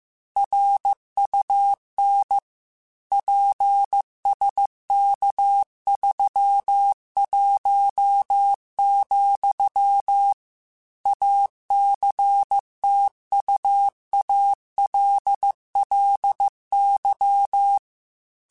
Morse code, heard as a keyed tone, about 15 wpm, 800 Hz. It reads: RUN PSK31, ACTUALLY